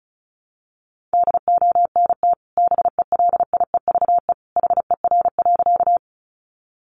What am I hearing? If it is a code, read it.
Morse code, heard as a keyed tone, about 34 wpm, 700 Hz. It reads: DONT BELIEVE HER.
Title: Sentences from Top 1000 Words 35wpm